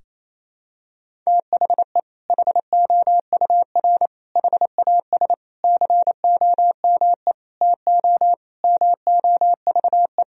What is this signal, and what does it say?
Morse code, heard as a keyed tone, about 28 wpm, 700 Hz. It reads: THE HOUR HAS COME TO MOVE
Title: Sentences from Top 1000 Words - Review Part 1 - 28wpm